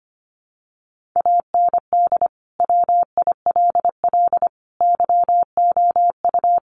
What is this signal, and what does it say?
Morse code, heard as a keyed tone, about 25 wpm, 700 Hz. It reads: AND WILL YOU
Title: Sentences from Top 1500 Words 25wpm